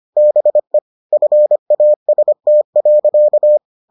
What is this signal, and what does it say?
Morse code, heard as a keyed tone, about 25 wpm, 600 Hz. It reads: BE FAST.